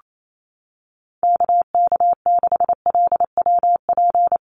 Morse code, heard as a keyed tone, about 28 wpm, 700 Hz. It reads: KK6LWP
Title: USA Call Signs Repeated 2x 28wpm